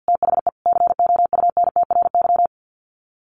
Morse code, heard as a keyed tone, about 50 wpm, 700 Hz. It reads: THE COUNTRY
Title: Sets of 5 Words Encore 50wpm